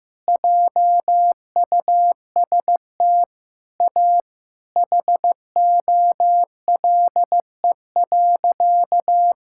Morse code, heard as a keyed tone, about 14 wpm, 700 Hz. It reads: JUST A HOLE.
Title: Sentences from Top 1600 Words 15wpm